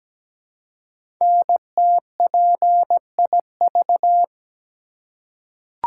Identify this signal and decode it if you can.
Morse code, heard as a keyed tone, about 17 wpm, 700 Hz. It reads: NTPIV
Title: Code Groups - 5 Letter:Numbers - 17wpm